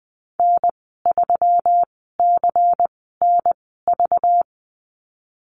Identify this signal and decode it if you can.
Morse code, heard as a keyed tone, about 20 wpm, spaced as for 15 wpm, 700 Hz. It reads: N3CNV
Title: 200 Lesson 32g - Letter k - Call Signs 20wpm